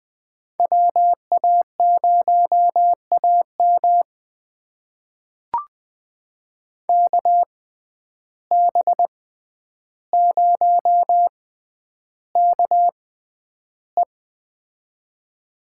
Morse code, heard as a keyed tone, about 20 wpm, spaced as for 10 wpm, 700 Hz. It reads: WA0AM KB0KE
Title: USA Call Signs - 5 Characters - 6x Character Spacing - 20wpm